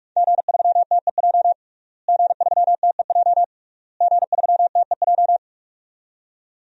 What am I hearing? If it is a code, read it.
Morse code, heard as a keyed tone, about 46 wpm, 700 Hz. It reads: G3TEJ G3TEJ G3TEJ
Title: DXCC-100 Call Signs - 5 Characters - Repeated 3x - 45wpm